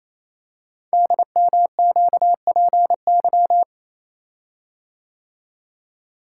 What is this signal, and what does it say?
Morse code, heard as a keyed tone, about 29 wpm, 700 Hz. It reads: DMQPY
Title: Code Groups - 5 Letters - 28wpm